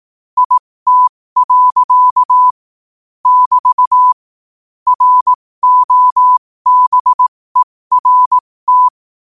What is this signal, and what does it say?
Morse code, heard as a keyed tone, about 18 wpm, spaced as for 15 wpm, 1000 Hz. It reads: IT. <BT> ROBERT
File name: qotdfiles / 15wpm